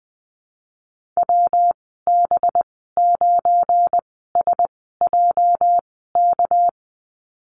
Morse code, heard as a keyed tone, about 20 wpm, spaced as for 15 wpm, 700 Hz. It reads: WB9SJK
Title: 213 Lesson 34g - Number 8 - Call Signs 20wpm